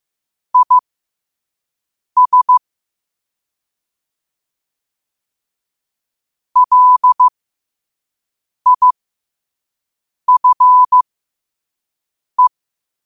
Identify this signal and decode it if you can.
Morse code, heard as a keyed tone, about 15 wpm, spaced as for 5 wpm, 1000 Hz. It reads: IS LIFE
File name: qotdfiles / 5wpm